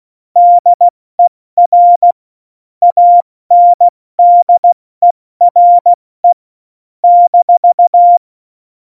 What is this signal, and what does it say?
Morse code, heard as a keyed tone, about 16 wpm, 700 Hz. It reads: DER ANDERE -
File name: Morsekurs / Witze / Witz 23. Juli 2019, 70 F80